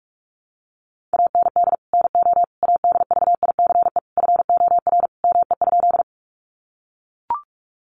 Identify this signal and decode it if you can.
Morse code, heard as a keyed tone, about 44 wpm, 700 Hz. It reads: AND NO ADVICE FOR ME?
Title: Sentences from Top 1100 Words 45wpm